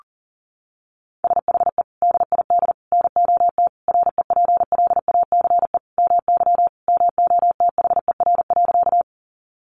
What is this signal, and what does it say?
Morse code, heard as a keyed tone, about 40 wpm, 700 Hz. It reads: SHE DID NOT REPLACE MY MOTHER.
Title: Sentences from Top 1400 Words 40wpm